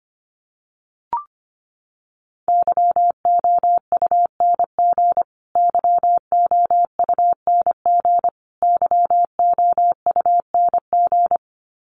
Morse code, heard as a keyed tone, about 25 wpm, 700 Hz. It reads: YOUNG YOUNG YOUNG
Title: Top 500 Words Repeated 25wpm